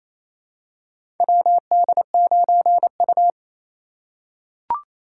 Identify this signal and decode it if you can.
Morse code, heard as a keyed tone, about 28 wpm, 700 Hz. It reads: WD9U
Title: USA Call Signs - 4 Characters - 3x Character Spacing - 28wpm